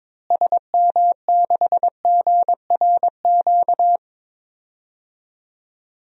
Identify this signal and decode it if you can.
Morse code, heard as a keyed tone, about 22 wpm, 700 Hz. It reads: SM6GRQ
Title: DXCC-100 Call Signs - 6 Characters - Repeated 2x - 22wpm